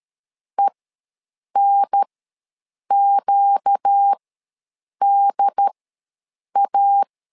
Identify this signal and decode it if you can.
Morse code, heard as a keyed tone, about 13 wpm, spaced as for 7 wpm, 800 Hz. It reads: ENQDA
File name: LEZIONI / 4 Q ws 12wpm 1912